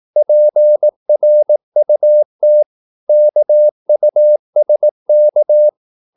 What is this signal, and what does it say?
Morse code, heard as a keyed tone, about 18 wpm, 600 Hz. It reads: PRUT KUSK
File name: Korte ord DK 18wpm